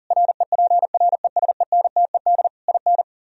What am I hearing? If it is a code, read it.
Morse code, heard as a keyed tone, about 52 wpm, 700 Hz. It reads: REPRESENTED IN